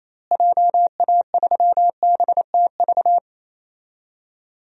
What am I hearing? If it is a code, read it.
Morse code, heard as a keyed tone, about 28 wpm, 700 Hz. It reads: JA3BTV
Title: DXCC-100 Call Signs - 6 Characters - Repeated 3x - 28wpm